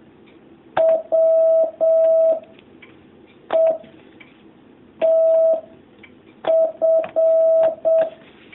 Morse code, heard as a keyed tone, about 7 wpm, 650 Hz. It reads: W E T F
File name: jh8cbh / 2019cw